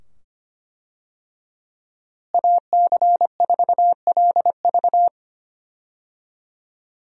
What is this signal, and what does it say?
Morse code, heard as a keyed tone, about 25 wpm, 700 Hz. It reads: AC4LV
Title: USA Call Signs - 5 Characters - Repeated 2x - 25wpm